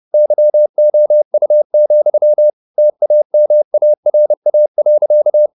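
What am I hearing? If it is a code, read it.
Morse code, heard as a keyed tone, about 30 wpm, 600 Hz. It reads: YOU, TAMARA.
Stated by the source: Morse code at 30 wpm, the novel Never by Ken Follett in English, from CW med Gnister.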